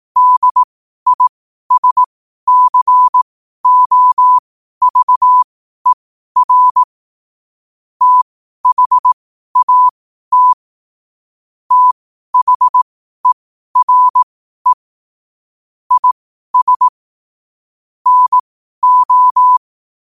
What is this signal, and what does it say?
Morse code, heard as a keyed tone, about 18 wpm, spaced as for 12 wpm, 1000 Hz. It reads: DISCOVER THAT THERE IS NO
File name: qotdfiles / 12wpm